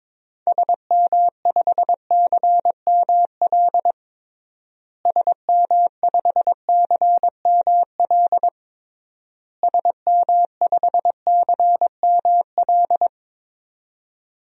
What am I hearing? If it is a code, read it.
Morse code, heard as a keyed tone, about 22 wpm, 700 Hz. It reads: SM5CML SM5CML SM5CML
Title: DXCC-100 Call Signs - 6 Characters - Repeated 3x - 22wpm